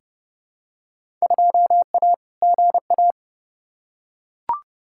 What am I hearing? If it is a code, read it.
Morse code, heard as a keyed tone, about 30 wpm, 700 Hz. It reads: 2A GA
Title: Field Day Exchanges Encore - 30wpm